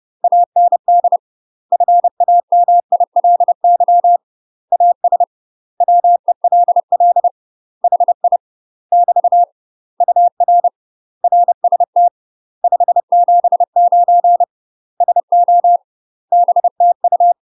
Morse code, heard as a keyed tone, about 30 wpm, 700 Hz. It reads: AND FAMILY AS WELL HI <BT> UR RST 579 SO BTU
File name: UK MASTER QSOs TRAINING / 30WPM / UK MASTER QSOs 0023